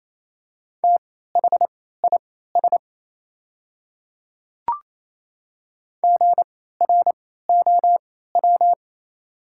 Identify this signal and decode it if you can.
Morse code, heard as a keyed tone, about 28 wpm, spaced as for 16 wpm, 700 Hz. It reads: THIS GROW
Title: 3x Character Spacing - Top 500 Words - 28wpm